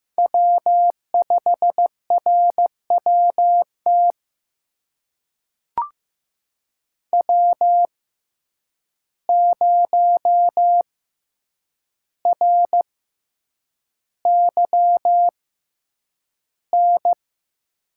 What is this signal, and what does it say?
Morse code, heard as a keyed tone, about 15 wpm, spaced as for 8 wpm, 700 Hz. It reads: W5RWT W0RYN
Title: USA Call Signs - 5 Characters - 6x Character Spacing - 15wpm